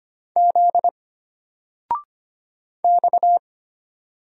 Morse code, heard as a keyed tone, about 25 wpm, 700 Hz. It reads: Z X
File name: Single-Letter-Number-25wpm